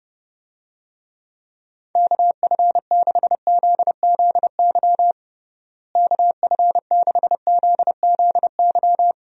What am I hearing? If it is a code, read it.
Morse code, heard as a keyed tone, about 30 wpm, 700 Hz. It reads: KF6ZZY KF6ZZY
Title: USA Call Signs - 6 Characters - Repeated 3x - 30wpm